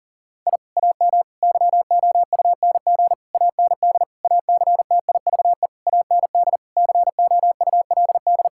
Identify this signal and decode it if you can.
Morse code, heard as a keyed tone, about 37 wpm, 700 Hz. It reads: I AM YOUNG AND ACTIVE AND COULD
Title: Sentences from Top 1000 Words - Review Part 6 - 40wpm